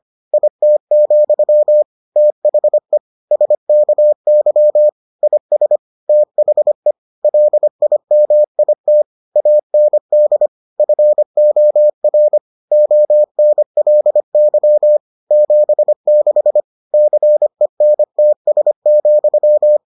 Morse code, heard as a keyed tone, about 25 wpm, 600 Hz. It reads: IT, THE SKY IS THE LIMIT AND FOR ONLY 76 CENTS,